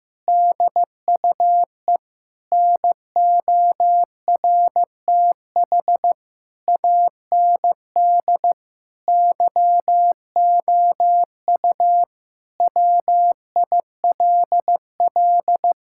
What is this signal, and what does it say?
Morse code, heard as a keyed tone, about 15 wpm, 700 Hz. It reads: DUE NORTH AND YOU WILL
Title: Sentences from Top 1000 Words - Review Part 6 - 15wpm